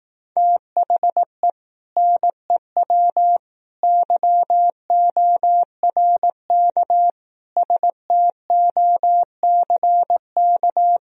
Morse code, heard as a keyed tone, about 18 wpm, 700 Hz. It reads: THE NEW YORK STOCK
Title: Speed-Racing - Sets of 5 Words 15wpm